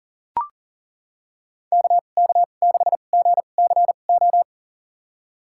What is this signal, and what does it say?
Morse code, heard as a keyed tone, about 40 wpm, spaced as for 29 wpm, 700 Hz. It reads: KK6GCO
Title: USA Call Signs - 6 Characters - 2x Character Spacing - 40wpm